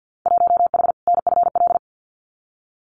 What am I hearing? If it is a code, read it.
Morse code, heard as a keyed tone, about 50 wpm, 700 Hz. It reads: 1H NFL
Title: Winter Field Day Exchanges 50wpm